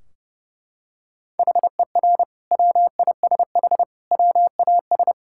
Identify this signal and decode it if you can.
Morse code, heard as a keyed tone, about 30 wpm, 700 Hz. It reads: HER WISH WAS
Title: Sentences from Top 1600 Words 30wpm